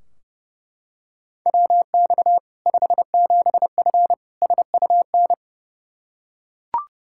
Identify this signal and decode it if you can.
Morse code, heard as a keyed tone, about 32 wpm, 700 Hz. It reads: WX 57F SUN
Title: K1USN SST QSO Practice 30wpm